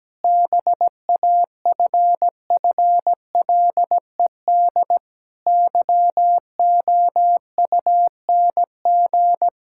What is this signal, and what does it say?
Morse code, heard as a keyed tone, about 17 wpm, 700 Hz. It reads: BAFFLED YOUNG
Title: Pangram Challenge v2 - 17wpm